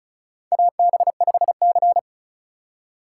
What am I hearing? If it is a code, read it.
Morse code, heard as a keyed tone, about 34 wpm, 700 Hz. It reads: AB5C
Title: K1USN SST QSO Practice - 35wpm